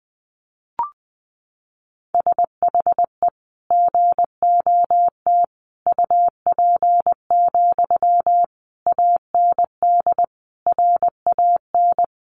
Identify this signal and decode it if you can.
Morse code, heard as a keyed tone, about 20 wpm, 700 Hz. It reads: SHE GOT UP, AND RAN
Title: Sentences from Top 500 Words 20wpm